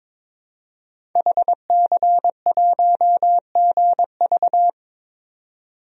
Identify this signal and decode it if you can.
Morse code, heard as a keyed tone, about 22 wpm, 700 Hz. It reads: HC1GV
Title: DXCC-100 Call Signs - 5 Characters - 22wpm